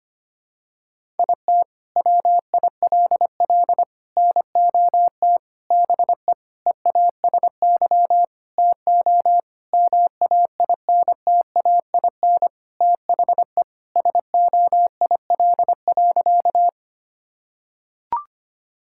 Morse code, heard as a keyed tone, about 25 wpm, 700 Hz. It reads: IT WILL NOT BE EASY TO MAINTAIN THE SOIL.
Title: Sentences from Top 1000 Words - Review Part 5 - 25wpm